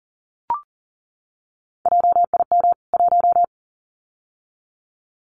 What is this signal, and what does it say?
Morse code, heard as a keyed tone, about 40 wpm, 700 Hz. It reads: JIM 1
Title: CWT Exchange 40wpm